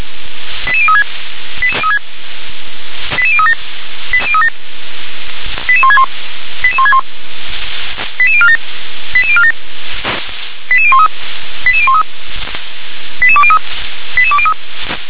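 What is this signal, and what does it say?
Five-tone selective call (ZVEI), heard as quick runs of five tones, tones of about 70 ms. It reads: 88036 88037 88161 88161 88047 88047 88013 88013 88203 88203
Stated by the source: ZVEI five-tone selective call received with an SDR.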